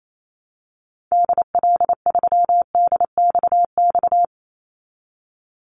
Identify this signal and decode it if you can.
Morse code, heard as a keyed tone, about 28 wpm, 700 Hz. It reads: DL3DXX
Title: Call Signs Repeated 3x - 28wpm